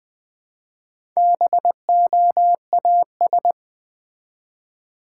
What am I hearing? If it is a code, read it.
Morse code, heard as a keyed tone, about 20 wpm, 700 Hz. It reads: BOAS
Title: Build A Word - 1 Letter At A Time - 5-Letter Words - 20wpm